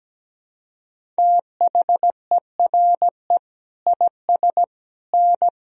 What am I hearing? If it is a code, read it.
Morse code, heard as a keyed tone, about 17 wpm, 700 Hz. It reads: THERE IS N
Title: Sentences from Top 500 Words and Abbreviations 17wpm